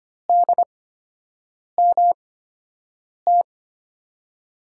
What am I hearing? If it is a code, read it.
Morse code, heard as a keyed tone, about 25 wpm, spaced as for 7 wpm, 700 Hz. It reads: DMT
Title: USA Call Signs - 5 Characters - 8x Character Spacing - 25wpm